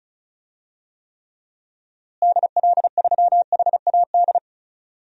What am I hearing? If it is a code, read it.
Morse code, heard as a keyed tone, about 36 wpm, 700 Hz. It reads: DL3HAD
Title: DXCC-100 Call Signs - 6 Characters - Repeated 2x - 35wpm